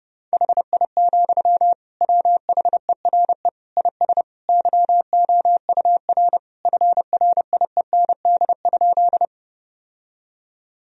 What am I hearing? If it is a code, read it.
Morse code, heard as a keyed tone, about 30 wpm, 700 Hz. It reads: HI, WHERE IS YOUR FRIEND?
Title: Sentences from Top 1000 Words - Review Part 3 - 30wpm